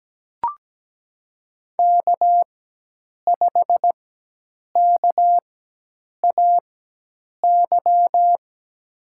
Morse code, heard as a keyed tone, about 17 wpm, spaced as for 8 wpm, 700 Hz. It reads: K5KAY
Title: USA Call Signs - 5 Characters - 4x Character Spacing - 17wpm